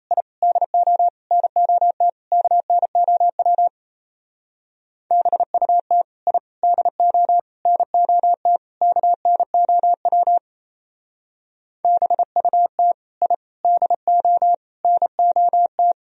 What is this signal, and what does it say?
Morse code, heard as a keyed tone, about 32 wpm, 700 Hz. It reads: I DO NOT KNOW BUT I DO NOT KNOW BUT I DO NOT
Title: Speed-Racing - Sets of 5 Words 28wpm